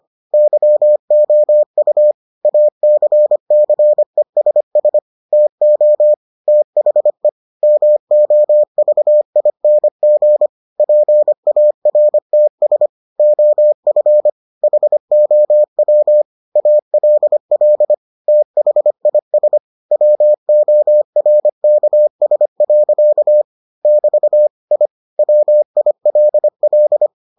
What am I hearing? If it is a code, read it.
Morse code, heard as a keyed tone, about 25 wpm, 600 Hz. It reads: YOU ACCESS TO THE MOVING PARTS OF HOW ALL THIS WORKS. <BT> I WILL